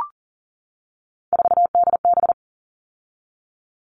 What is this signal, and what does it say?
Morse code, heard as a keyed tone, about 40 wpm, 700 Hz. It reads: 4DB
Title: All Cut Numbers - 3-Digit 40wpm